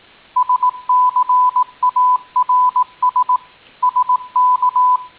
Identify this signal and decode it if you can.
Morse code, heard as a keyed tone, about 18 wpm, 1000 Hz. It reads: SCARS SK
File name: tin-upgrade / QSO 2019 08 13